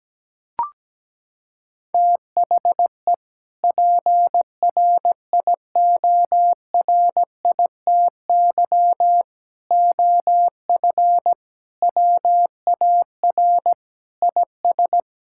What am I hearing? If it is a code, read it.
Morse code, heard as a keyed tone, about 17 wpm, 700 Hz. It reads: THE PRIORITY OF WAR IS
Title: Sentences from Top 1600 Words 17wpm